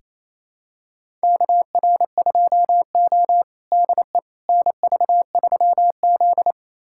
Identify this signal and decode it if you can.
Morse code, heard as a keyed tone, about 28 wpm, 700 Hz. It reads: KR2O DE NV3Z
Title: USA Callsign DE Callsign - 4 Characters - 28wpm